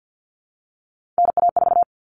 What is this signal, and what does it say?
Morse code, heard as a keyed tone, about 50 wpm, 700 Hz. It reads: NA4
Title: All Cut Numbers - 3-Digit 50wpm